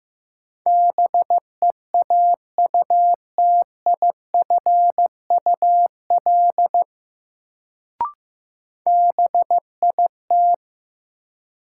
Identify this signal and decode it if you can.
Morse code, heard as a keyed tone, about 16 wpm, 700 Hz. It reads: BEAUTIFUL BIT
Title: Sentences from Top 900 Words - New Word Drill - 15wpm